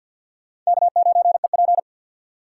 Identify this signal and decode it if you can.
Morse code, heard as a keyed tone, about 62 wpm, 700 Hz. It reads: K9EP
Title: Speed Racing - USA 1x2 Call Signs 50wpm